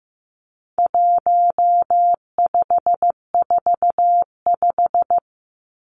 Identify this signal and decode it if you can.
Morse code, heard as a keyed tone, about 15 wpm, 700 Hz. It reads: 1545
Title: Numbers - 4-Digit 15wpm